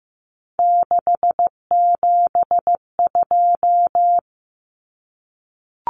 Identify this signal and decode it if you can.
Morse code, heard as a keyed tone, about 15 wpm, 700 Hz. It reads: 672
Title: Numbers - 3-Digit 15wpm